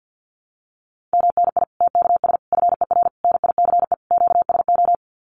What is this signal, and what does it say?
Morse code, heard as a keyed tone, about 50 wpm, 700 Hz. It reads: MNI TKS FER NICE QSO
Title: Common QSO Phrases 50wpm